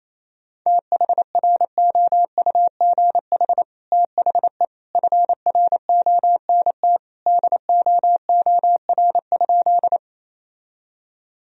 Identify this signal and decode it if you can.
Morse code, heard as a keyed tone, about 28 wpm, 700 Hz. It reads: THROUGH THE FRONT DOOR?
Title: Sentences from Top 1000 Words - Review Part 2 - 28wpm